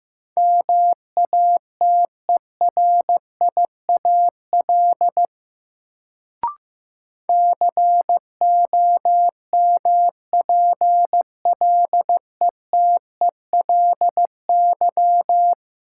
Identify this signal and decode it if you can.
Morse code, heard as a keyed tone, about 15 wpm, 700 Hz. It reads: MATERIAL COMPLETELY
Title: Sentences from Top 1100 Words - New Word Drill Alt - 15wpm